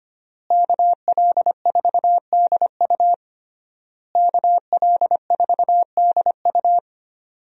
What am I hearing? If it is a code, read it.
Morse code, heard as a keyed tone, about 25 wpm, 700 Hz. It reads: KL4DU KL4DU
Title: USA Call Signs - 5 Characters - Repeated 3x - 25wpm